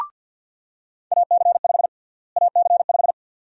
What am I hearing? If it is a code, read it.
Morse code, heard as a keyed tone, about 45 wpm, 700 Hz. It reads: AC5 AC5
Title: USA Call Sign Prefixes - Repeated 2x - 50wpm